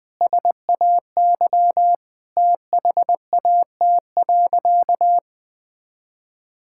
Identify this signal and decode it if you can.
Morse code, heard as a keyed tone, about 20 wpm, 700 Hz. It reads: SAY THAT.
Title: Sentences with 5-Word Phrases - 20wpm